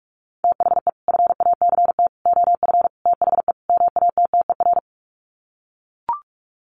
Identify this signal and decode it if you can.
Morse code, heard as a keyed tone, about 44 wpm, 700 Hz. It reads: THE FACT OF THE MATTER
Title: Sets of 5 Words 45wpm